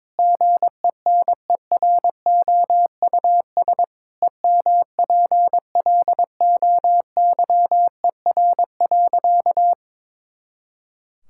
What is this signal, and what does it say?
Morse code, heard as a keyed tone, about 22 wpm, 700 Hz. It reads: GENEROUS EMPLOYER.
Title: Pangram Challenge v2 - 22wpm